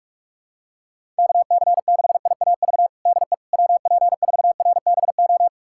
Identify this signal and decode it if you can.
Morse code, heard as a keyed tone, about 46 wpm, 700 Hz. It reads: KC6IAV DE WP4RBO
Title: USA Callsign DE Callsign 45wpm